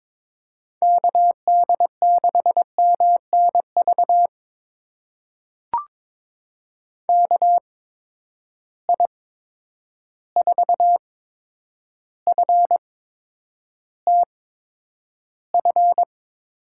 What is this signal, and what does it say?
Morse code, heard as a keyed tone, about 22 wpm, spaced as for 9 wpm, 700 Hz. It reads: KD6MNV KI4FTF
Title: USA Call Signs - 6 Characters - 8x Character Spacing - 22wpm